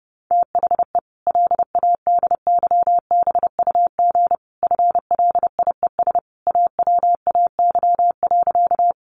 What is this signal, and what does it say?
Morse code, heard as a keyed tone, about 30 wpm, 700 Hz. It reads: THE LADYBUG FLIES AWAY.
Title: Additional Easy Phrases 30wpm